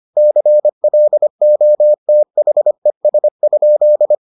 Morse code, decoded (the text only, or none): CLOTHES?